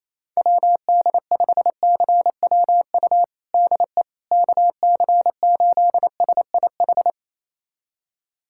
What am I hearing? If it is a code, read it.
Morse code, heard as a keyed tone, about 28 wpm, 700 Hz. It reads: WD5CWU DE KC8SIH
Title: USA Callsign DE Callsign 28wpm